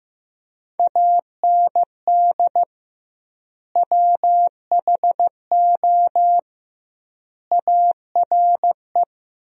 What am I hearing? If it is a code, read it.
Morse code, heard as a keyed tone, about 15 wpm, 700 Hz. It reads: AND WHO ARE
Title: 2x Word-Spacing - Sentences from Top 100 Words - 15wpm